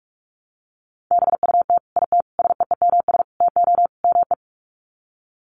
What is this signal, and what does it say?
Morse code, heard as a keyed tone, about 44 wpm, 700 Hz. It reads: BUT IT SEEMS TO ME
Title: Sets of 5 Words Encore2 45wpm